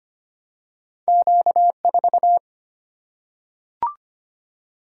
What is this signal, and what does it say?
Morse code, heard as a keyed tone, about 25 wpm, 700 Hz. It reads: Q 4
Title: Single-Letter Single-Number 25wpm